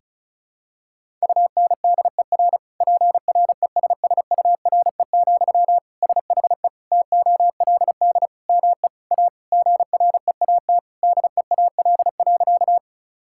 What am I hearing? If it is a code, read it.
Morse code, heard as a keyed tone, about 34 wpm, 700 Hz. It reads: UNDER PRESSURE, SHE TOLD ME A GREAT DEAL.
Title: Sentences from Top 1000 Words - Review Part 6 - 35wpm